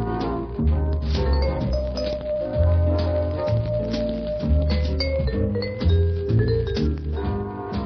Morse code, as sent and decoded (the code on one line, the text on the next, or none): none